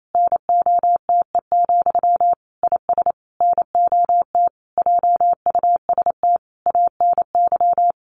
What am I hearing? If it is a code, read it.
Morse code, heard as a keyed tone, about 28 wpm, 700 Hz. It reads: NOTE, IS NOT JUST ANY
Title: A Sound Of Thunder 28wpm